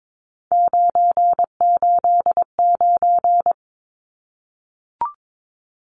Morse code, heard as a keyed tone, about 22 wpm, 700 Hz. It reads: 989
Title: Numbers - 3-Digit 22wpm